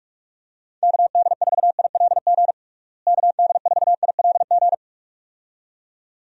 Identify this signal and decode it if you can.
Morse code, heard as a keyed tone, about 46 wpm, 700 Hz. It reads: KD4ILG KD4ILG
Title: USA Call Signs - 6 Characters - Repeated 2x - 45wpm